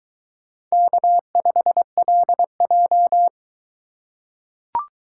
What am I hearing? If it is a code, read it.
Morse code, heard as a keyed tone, about 22 wpm, 700 Hz. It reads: K5LJ